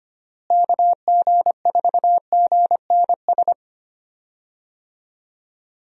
Morse code, heard as a keyed tone, about 25 wpm, 700 Hz. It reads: KG4GNS
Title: USA Call Signs - 6 Characters - Repeated 2x - 25wpm